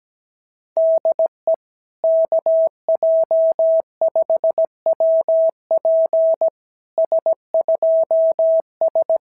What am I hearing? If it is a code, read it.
Morse code, heard as a keyed tone, about 16 wpm, 650 Hz. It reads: DE KJ5WP S2S
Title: SOTA QSO Practice 17wpm